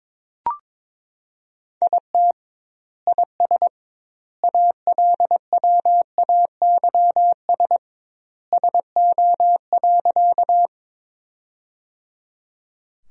Morse code, heard as a keyed tone, about 21 wpm, 700 Hz. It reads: IT IS ALWAYS SO.
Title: 2x Word-Spacing - Sentences from Top 300 Words - 22wpm